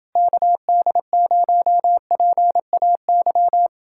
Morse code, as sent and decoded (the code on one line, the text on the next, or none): -.- -.. ----- .--. .- -.--
KD0PAY